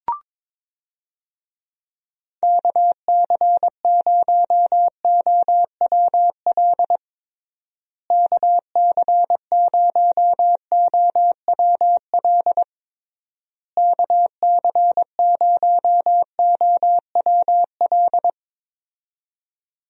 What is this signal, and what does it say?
Morse code, heard as a keyed tone, about 22 wpm, 700 Hz. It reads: KC0OWL KC0OWL KC0OWL
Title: USA Call Signs - 6 Characters - Repeated 3x - 22wpm